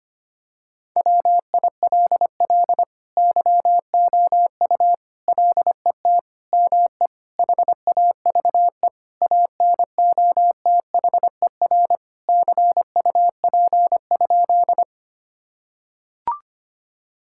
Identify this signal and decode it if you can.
Morse code, heard as a keyed tone, about 25 wpm, 700 Hz. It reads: WILL YOU LET ME HAVE ANOTHER CUP?
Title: Sentences from Top 1600 Words 25wpm